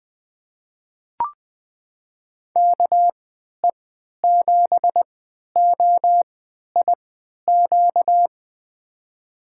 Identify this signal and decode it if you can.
Morse code, heard as a keyed tone, about 20 wpm, spaced as for 11 wpm, 700 Hz. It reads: KE7OIQ
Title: USA Call Signs - 6 Characters - 3x Character Spacing - 20wpm